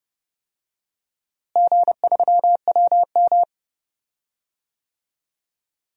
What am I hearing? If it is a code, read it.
Morse code, heard as a keyed tone, about 30 wpm, 700 Hz. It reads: G3WM
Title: DXCC-100 Call Signs - 4 Characters - Repeated 2x - 30wpm